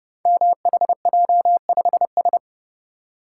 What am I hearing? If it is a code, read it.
Morse code, heard as a keyed tone, about 30 wpm, 700 Hz. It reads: MHJ5S
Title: Code Groups - 5 Letter:Numbers - 30wpm